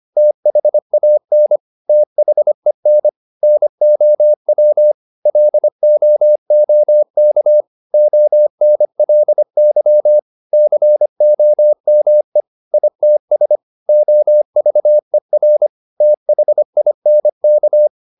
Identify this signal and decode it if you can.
Morse code, heard as a keyed tone, about 25 wpm, 600 Hz. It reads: THAN THEN NOW LOOK ONLY COME ITS OVER THINK
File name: Top-100-Words-25